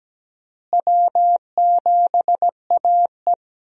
Morse code, heard as a keyed tone, about 17 wpm, 700 Hz. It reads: W7AE
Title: USA Call Signs - 4 Characters - Repeated 2x - 17wpm